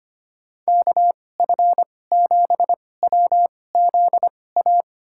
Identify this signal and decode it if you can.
Morse code, heard as a keyed tone, about 25 wpm, spaced as for 18 wpm, 700 Hz. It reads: KF7WZA
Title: USA Call Signs - 6 Characters - 2x Character Spacing - 25wpm